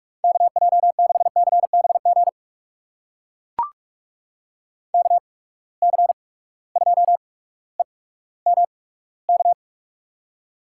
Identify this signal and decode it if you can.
Morse code, heard as a keyed tone, about 43 wpm, spaced as for 19 wpm, 700 Hz. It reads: KJ6CBG KC2EMK
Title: USA Call Signs - 6 Characters - 8x Character Spacing - 45wpm